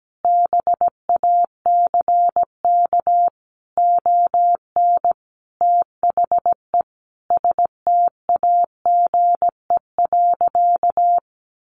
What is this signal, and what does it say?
Morse code, heard as a keyed tone, about 17 wpm, 700 Hz. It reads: BACK ON THE STAGE.
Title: Sentences from Top 1200 Words 17wpm